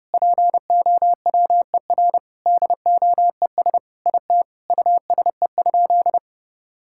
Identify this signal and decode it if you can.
Morse code, heard as a keyed tone, about 30 wpm, 700 Hz. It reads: POWER DOES IT USE?
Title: Sentences from Top 1000 Words - Review Part 6 - 30wpm